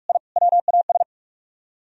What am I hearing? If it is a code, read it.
Morse code, heard as a keyed tone, about 59 wpm, 700 Hz. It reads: I WAS